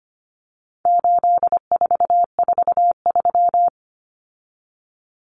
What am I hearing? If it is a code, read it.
Morse code, heard as a keyed tone, about 25 wpm, 700 Hz. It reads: 8443
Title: Common Cut Numbers - 4-Digit 25wpm